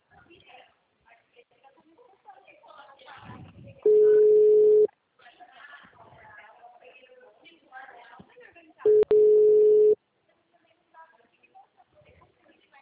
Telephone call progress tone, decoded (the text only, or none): ringback tone